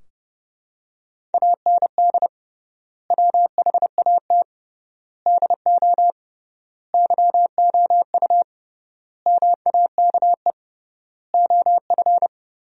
Morse code, heard as a keyed tone, about 30 wpm, 700 Hz. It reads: AND WHAT DO YOU MAKE OF
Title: 3x Word-Spacing - Sentences from Top 100 Words - 30wpm